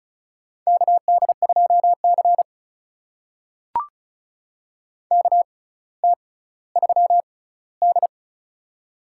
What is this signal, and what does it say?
Morse code, heard as a keyed tone, about 36 wpm, spaced as for 18 wpm, 700 Hz. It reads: KD2C KT3D
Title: USA Call Signs - 4 Characters - 6x Character Spacing - 35wpm